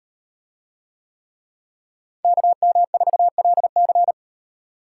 Morse code, heard as a keyed tone, about 38 wpm, 700 Hz. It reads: KM4LC